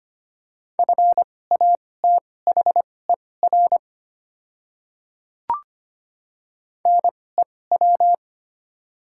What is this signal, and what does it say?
Morse code, heard as a keyed tone, about 25 wpm, spaced as for 18 wpm, 700 Hz. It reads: FATHER NEW
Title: 2x Character Spacing - Top 200 Words - 25wpm